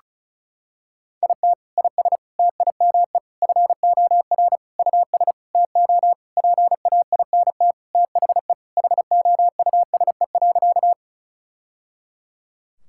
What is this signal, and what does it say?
Morse code, heard as a keyed tone, about 34 wpm, 700 Hz. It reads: IT IS TIME FOR US TO PAINT THE HOUSE.
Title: Sentences from Top 1000 Words - Review Part 6 - 35wpm